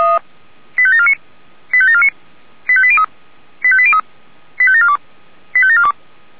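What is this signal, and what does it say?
Five-tone selective call (ZVEI), heard as quick runs of five tones, tones of about 80 ms. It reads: 85749 85749 85803 85803 85742 85742
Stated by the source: ZVEI five-tone selective call received on SDR.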